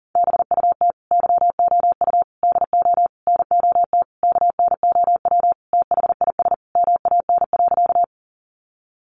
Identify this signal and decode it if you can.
Morse code, heard as a keyed tone, about 37 wpm, 700 Hz. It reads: BUT YOU DO NOT KNOW THIS MAN.
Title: Sentences from Top 600 Words 40wpm